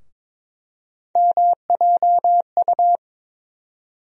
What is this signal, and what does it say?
Morse code, heard as a keyed tone, about 22 wpm, 700 Hz. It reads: MJU
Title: USA Call Sign Postfixes - 3 Letters 22wpm